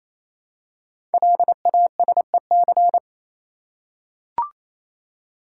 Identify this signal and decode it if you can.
Morse code, heard as a keyed tone, about 29 wpm, 690 Hz. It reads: LASEC
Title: Code Groups - 5 Letters - 28wpm